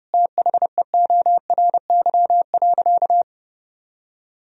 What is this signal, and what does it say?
Morse code, heard as a keyed tone, about 32 wpm, 700 Hz. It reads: THEORY.
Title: Sentences from Top 1000 Words - Review Part 3 - 30wpm